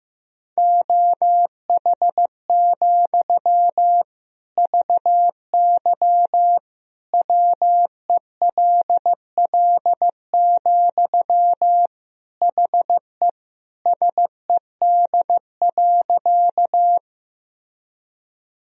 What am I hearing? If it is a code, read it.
Morse code, heard as a keyed tone, about 14 wpm, 700 Hz. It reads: OH, VY WELL, HE SED.
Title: Sentences from Top 500 Words and Abbreviations 15wpm